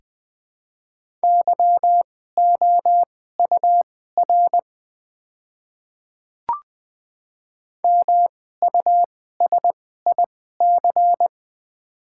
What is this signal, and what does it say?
Morse code, heard as a keyed tone, about 20 wpm, spaced as for 14 wpm, 700 Hz. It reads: YOUR MUSIC
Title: 2x Character Spacing - Top 500 Words - 20wpm